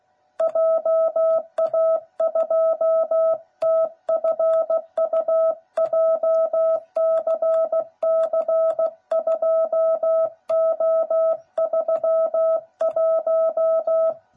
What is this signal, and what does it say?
Morse code, heard as a keyed tone, about 16 wpm, 660 Hz. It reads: JA2TFU JCC2O31
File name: jh8cbh / 2021cw / cw09b